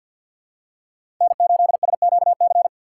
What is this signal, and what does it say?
Morse code, heard as a keyed tone, about 46 wpm, 690 Hz. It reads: N8SQC
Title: USA Call Signs - 5 Characters - 6x Character Spacing - 50wpm